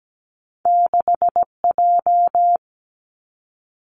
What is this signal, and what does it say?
Morse code, heard as a keyed tone, about 17 wpm, 700 Hz. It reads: 6J
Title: Two Challenging Letter-Number Combinations 17wpm